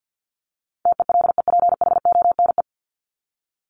Morse code, heard as a keyed tone, about 50 wpm, 700 Hz. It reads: TELEPHONE
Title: Top 1501-1750 Words 50wpm